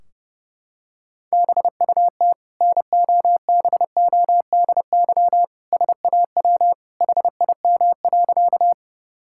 Morse code, heard as a keyed tone, about 30 wpm, 700 Hz. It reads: BUT NOBODY SAW HIM.
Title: Sentences from Top 1600 Words 30wpm